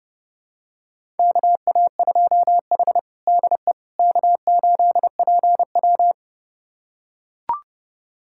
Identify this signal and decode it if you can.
Morse code, heard as a keyed tone, about 30 wpm, 700 Hz. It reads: KA2H DE K8PW
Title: USA Callsign DE Callsign - 4 Characters - 30wpm